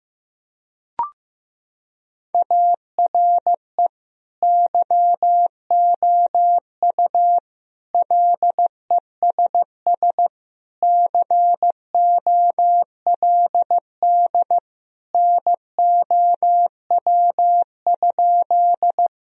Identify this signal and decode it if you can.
Morse code, heard as a keyed tone, about 15 wpm, 700 Hz. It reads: ARE YOU LESS COLD NOW?
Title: Sentences from Top 1000 Words - Review Part 2 - 15wpm